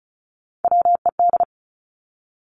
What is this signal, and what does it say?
Morse code, heard as a keyed tone, about 36 wpm, 700 Hz. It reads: WED
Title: Three Challenging Letter-Number Combinations 35wpm